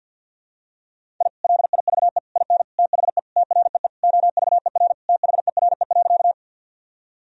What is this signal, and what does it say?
Morse code, heard as a keyed tone, about 47 wpm, 700 Hz. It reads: I LIVE IN THE TREE OVER THERE.
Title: Sentences from Top 1000 Words - Review Part 1 - 50wpm